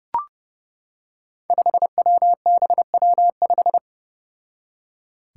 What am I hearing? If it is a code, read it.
Morse code, heard as a keyed tone, about 28 wpm, 700 Hz. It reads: 5WBW5
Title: Five Challenging Letter-Number Combinations 30wpm